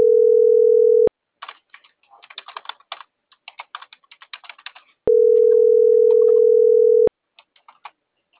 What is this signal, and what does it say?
Ringback tone (North America), 440 + 480 Hz.